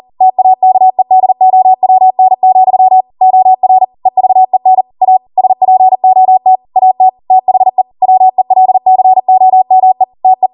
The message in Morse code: TAKEDOWN, OR EVEN A SPOT AT THE WELCOME TE